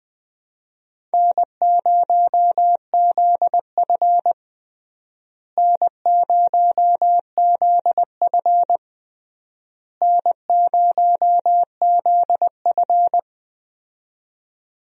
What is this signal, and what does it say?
Morse code, heard as a keyed tone, about 21 wpm, 700 Hz. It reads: N0ZF N0ZF N0ZF
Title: USA Call Signs - 4 Characters - Repeated 3x - 20wpm